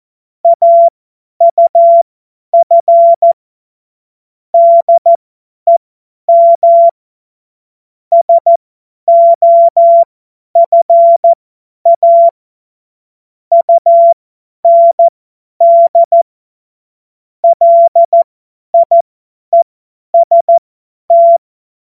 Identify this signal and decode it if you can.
Morse code, heard as a keyed tone, about 14 wpm, spaced as for 10 wpm, 680 Hz. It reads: AUF DEM SOFA UND LIEST
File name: Morsekurs / Witze / Witz 11. Januar 2022, 50 F70